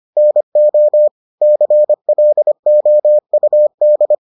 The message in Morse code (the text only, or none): NO CLOUD